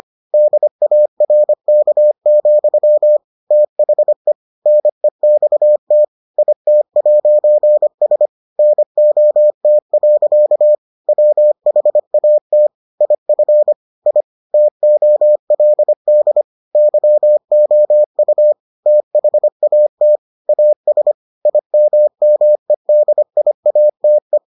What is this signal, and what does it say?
Morse code, heard as a keyed tone, about 25 wpm, 600 Hz. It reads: DARK, THE NEXT IT'S NOT. WHAT IF I TOLD YOU THAT AS IMMEDIATE